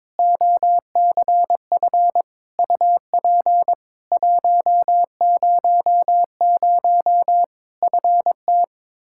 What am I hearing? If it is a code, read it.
Morse code, heard as a keyed tone, about 21 wpm, 700 Hz. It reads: OCF UP 100 FT
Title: Practice QSO - Exchange 2 - Info - 22wpm